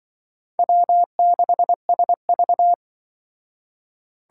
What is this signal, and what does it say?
Morse code, heard as a keyed tone, about 23 wpm, 700 Hz. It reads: W6SV
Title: Speed Racing - USA 1x2 Call Signs 20wpm